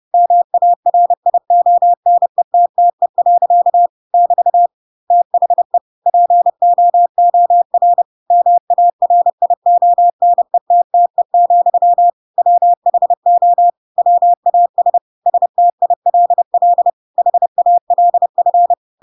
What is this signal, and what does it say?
Morse code, heard as a keyed tone, about 30 wpm, 700 Hz. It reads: MARIONETTE. <BT> THE POOR MARIONETTE, WHO WAS STILL HALF
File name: Pinocchio / Pinocchio ch7 30 30 ik1hge